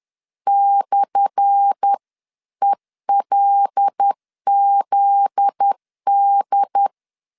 Morse code, heard as a keyed tone, about 10 wpm, 800 Hz. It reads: /ELZD
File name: LEZIONI / 10 SIGNS / 10B INT 10wpm 2844